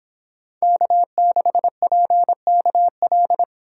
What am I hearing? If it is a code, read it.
Morse code, heard as a keyed tone, about 25 wpm, 700 Hz. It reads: K6PKL